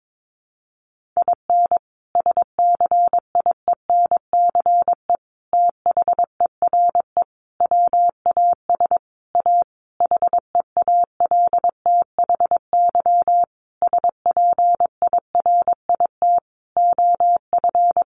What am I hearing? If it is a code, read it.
Morse code, heard as a keyed tone, about 22 wpm, 700 Hz. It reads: IN SCIENCE THERE WAS A HEALTHY SPIRIT OF
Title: Sentences from Top 1400 Words 22wpm